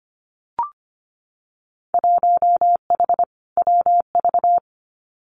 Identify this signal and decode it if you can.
Morse code, heard as a keyed tone, about 25 wpm, 700 Hz. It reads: 1H WV
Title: Winter Field Day Exchanges 25wpm